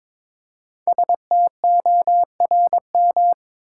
Morse code, heard as a keyed tone, about 22 wpm, 690 Hz. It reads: STORM
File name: 5 Letter Words - First 3 Letters Given - 22wpm